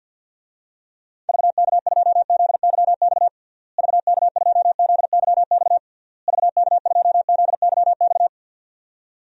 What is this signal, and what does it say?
Morse code, heard as a keyed tone, about 48 wpm, 700 Hz. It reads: VK2ZYX VK2ZYX VK2ZYX
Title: DXCC-100 Call Signs - 6 Characters - Repeated 3x - 50wpm